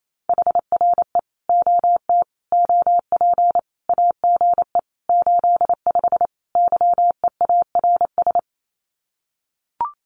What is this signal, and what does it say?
Morse code, heard as a keyed tone, about 28 wpm, 700 Hz. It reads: HRE OT OP AGE 85 YEARS
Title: Common QSO Phrases 28wpm